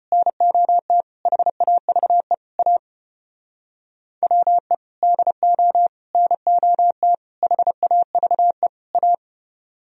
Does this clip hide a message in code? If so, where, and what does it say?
Yes: Morse code from 0.1 to 9.1 s, heard as a keyed tone, about 33 wpm, 700 Hz. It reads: NOT HAVE A WE DO NOT HAVE A